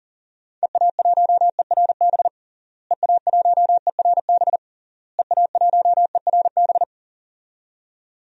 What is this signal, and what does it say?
Morse code, heard as a keyed tone, about 40 wpm, 700 Hz. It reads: EA1ERB EA1ERB EA1ERB
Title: DXCC-100 Call Signs - 6 Characters - Repeated 3x - 40wpm